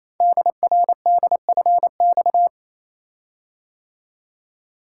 Morse code, heard as a keyed tone, about 29 wpm, 700 Hz. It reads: DRDFX
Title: Code Groups - 5 Letters - 28wpm